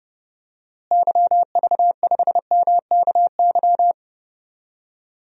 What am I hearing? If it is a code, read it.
Morse code, heard as a keyed tone, about 30 wpm, 700 Hz. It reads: YV5MKY
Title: DXCC-100 Call Signs - 6 Characters - Repeated 3x - 30wpm